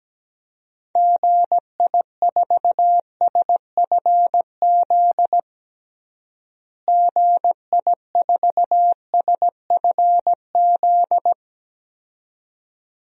Morse code, heard as a keyed tone, about 17 wpm, 700 Hz. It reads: GI4SFZ GI4SFZ
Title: DXCC-100 Call Signs - 6 Characters - Repeated 2x - 17wpm